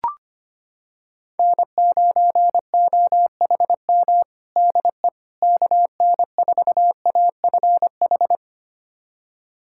Morse code, heard as a keyed tone, about 25 wpm, 700 Hz. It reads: N9OHM DE KN4AFH
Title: USA Callsign DE Callsign 25wpm